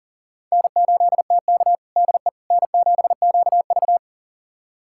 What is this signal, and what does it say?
Morse code, heard as a keyed tone, about 40 wpm, 700 Hz. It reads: N8TK DE N7QV
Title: USA Callsign DE Callsign - 4 Characters - 40wpm